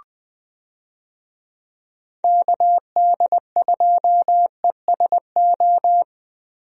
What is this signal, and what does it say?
Morse code, heard as a keyed tone, about 20 wpm, 700 Hz. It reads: KD2ESO
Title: USA Call Signs - 6 Characters - Repeated 3x - 20wpm